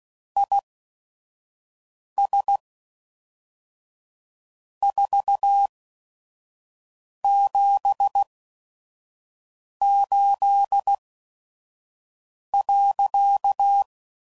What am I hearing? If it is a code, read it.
Morse code, heard as a keyed tone, about 16 wpm, spaced as for 5 wpm, 800 Hz. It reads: IS 478.
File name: CW Code Groups @5WPM / QSOs / QSO WD0FBT